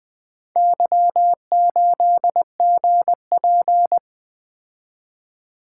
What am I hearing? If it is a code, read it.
Morse code, heard as a keyed tone, about 20 wpm, 700 Hz. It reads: Y8GP
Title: Four Letter-Number Combinations - 20wpm